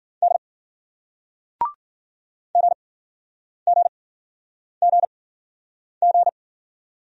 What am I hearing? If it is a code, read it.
Morse code, heard as a keyed tone, about 50 wpm, spaced as for 9 wpm, 700 Hz. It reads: D GGGG